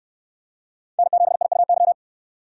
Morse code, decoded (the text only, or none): N7IUY